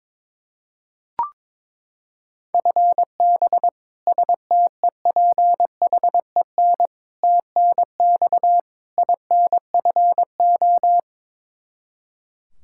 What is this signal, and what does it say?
Morse code, heard as a keyed tone, about 22 wpm, 700 Hz. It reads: FB STEPHEN TNX INFO
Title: Practice QSO - Exchange 3 - More Info - 22wpm